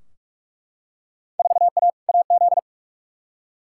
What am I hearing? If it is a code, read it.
Morse code, heard as a keyed tone, about 48 wpm, 700 Hz. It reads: 4A AZ
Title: Field Day Exchanges - 45wpm